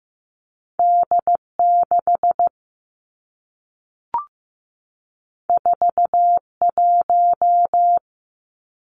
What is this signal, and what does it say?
Morse code, heard as a keyed tone, about 15 wpm, 700 Hz. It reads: D 6 4 1
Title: Two Challenging Letter-Number Combinations 15wpm